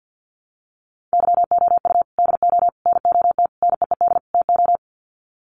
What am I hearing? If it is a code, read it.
Morse code, heard as a keyed tone, about 50 wpm, 700 Hz. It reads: YOU DO NOT NEED TO
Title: Sets of 5 Words 50wpm